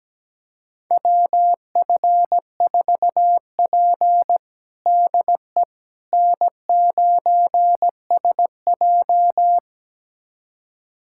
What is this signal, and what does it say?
Morse code, heard as a keyed tone, about 17 wpm, 700 Hz. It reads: WF4P DE N9SJ
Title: USA Callsign DE Callsign - 4 Characters - 17wpm